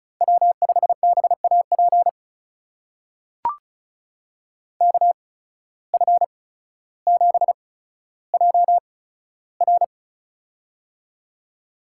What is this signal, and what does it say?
Morse code, heard as a keyed tone, about 36 wpm, spaced as for 15 wpm, 700 Hz. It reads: W5BAP KF7JR
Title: USA Call Signs - 5 Characters - 8x Character Spacing - 35wpm